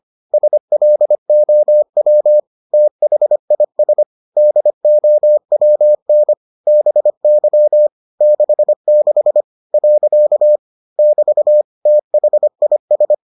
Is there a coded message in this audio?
Yes: Morse code from 0.3 to 13.1 s, heard as a keyed tone, about 25 wpm, 600 Hz. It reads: SLOW THIS DOWN BY 66 . <BT> THIS